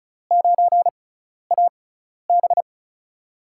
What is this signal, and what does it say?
Morse code, heard as a keyed tone, about 34 wpm, spaced as for 12 wpm, 700 Hz. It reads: 9AB
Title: USA Call Signs - 5 Characters - 6x Character Spacing - 35wpm